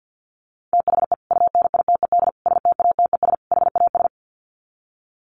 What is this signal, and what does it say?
Morse code, heard as a keyed tone, about 51 wpm, 700 Hz. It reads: THE UNITED STATES HAS
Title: Sets of 4 Words 50wpm